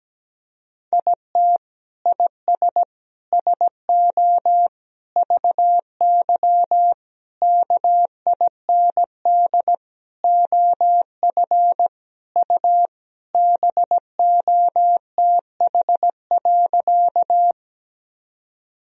Morse code, heard as a keyed tone, about 17 wpm, 700 Hz. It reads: IT IS SO VY KIND OF U BOTH.
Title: Sentences from Top 300 Words and Abbreviations 17wpm